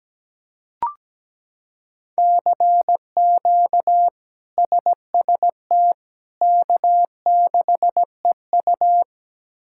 Morse code, heard as a keyed tone, about 17 wpm, 700 Hz. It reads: CQ SST K6EU
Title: K1USN SST QSO Practice - 17wpm